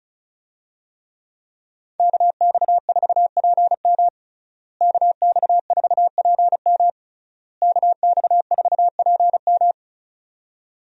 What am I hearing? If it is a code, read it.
Morse code, heard as a keyed tone, about 36 wpm, 700 Hz. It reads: KX4PM KX4PM KX4PM
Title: USA Call Signs - 5 Characters - Repeated 3x - 35wpm